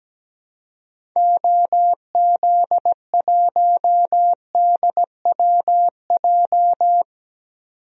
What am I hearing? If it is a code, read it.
Morse code, heard as a keyed tone, about 17 wpm, 700 Hz. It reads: OZ1DWJ
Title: DXCC-100 Call Signs - 6 Characters - 17wpm